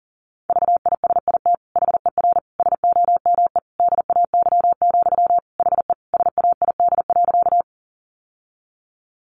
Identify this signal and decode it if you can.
Morse code, heard as a keyed tone, about 40 wpm, 700 Hz. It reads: VISIT HER SOME DAY, HE SAID.
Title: Sentences from Top 900 Words 40wpm